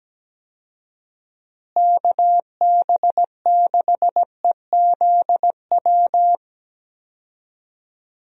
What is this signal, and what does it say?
Morse code, heard as a keyed tone, about 17 wpm, 700 Hz. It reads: KB6EZW
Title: USA Call Signs - 6 Characters - Repeated 2x - 17wpm